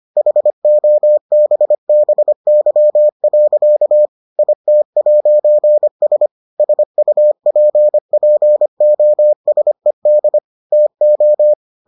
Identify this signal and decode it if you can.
Morse code, heard as a keyed tone, about 25 wpm, 600 Hz. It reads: HOBBY. IT'S SUPPOSED TO